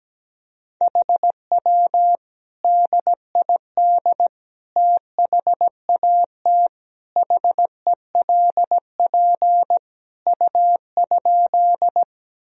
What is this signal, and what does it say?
Morse code, heard as a keyed tone, about 17 wpm, 700 Hz. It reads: HW DID THAT HELP U?
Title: Sentences from Top 200 Words and Abbreviations 17wpm